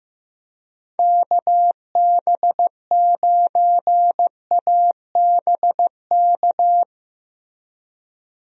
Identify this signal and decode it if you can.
Morse code, heard as a keyed tone, about 15 wpm, 700 Hz. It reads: KB9ABK
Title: USA Call Signs - 6 Characters - Repeated 3x - 15wpm